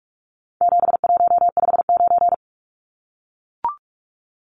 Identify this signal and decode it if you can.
Morse code, heard as a keyed tone, about 44 wpm, 700 Hz. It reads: 7159
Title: Numbers - 4-Digit 45wpm